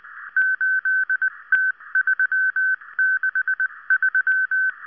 Morse code, heard as a keyed tone, about 19 wpm, 1550 Hz. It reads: 8T363